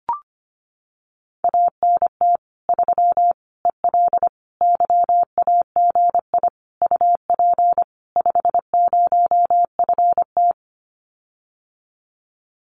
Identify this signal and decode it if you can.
Morse code, heard as a keyed tone, about 25 wpm, 700 Hz. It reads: ANT 3 EL YAGI UP 50FT
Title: Common QSO Phrases Encore 25wpm